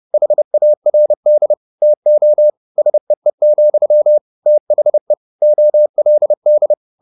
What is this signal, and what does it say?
Morse code, heard as a keyed tone, about 30 wpm, 600 Hz. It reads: HARD TO SEE, THE OLD